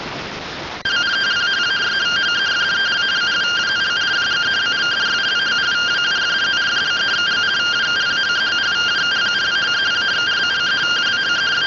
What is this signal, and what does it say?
Two-tone teleprinter signal, about 170 Hz shift.